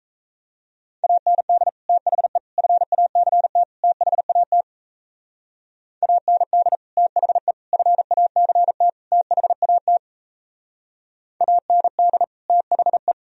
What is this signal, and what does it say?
Morse code, heard as a keyed tone, about 35 wpm, 700 Hz. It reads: AND THE FACT THAT AND THE FACT THAT AND THE
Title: Speed-Racing - Sets of 4 Words 28wpm